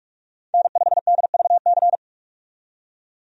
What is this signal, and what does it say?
Morse code, heard as a keyed tone, about 41 wpm, 700 Hz. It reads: N5DVC
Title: USA Call Signs - 5 Characters - 45wpm